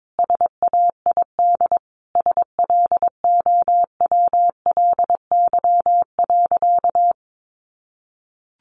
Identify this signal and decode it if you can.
Morse code, heard as a keyed tone, about 22 wpm, 700 Hz. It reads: SAID SLOWLY.
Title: Sentences from Top 1100 Words 22wpm